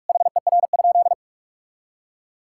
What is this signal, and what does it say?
Morse code, heard as a keyed tone, about 46 wpm, 700 Hz. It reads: HER?
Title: Sentences from Top 1000 Words - Review Part 6 - 45wpm